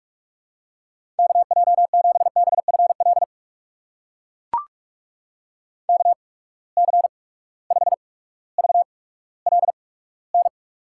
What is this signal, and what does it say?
Morse code, heard as a keyed tone, about 44 wpm, spaced as for 19 wpm, 700 Hz. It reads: KJ7BFL KC5VLN
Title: USA Call Signs - 6 Characters - 8x Character Spacing - 45wpm